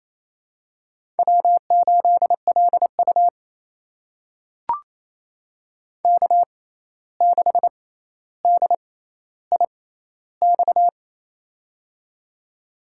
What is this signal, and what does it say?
Morse code, heard as a keyed tone, about 27 wpm, spaced as for 13 wpm, 700 Hz. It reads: W8LU K6DIX
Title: USA Call Signs - 6x Character Spacing - 28wpm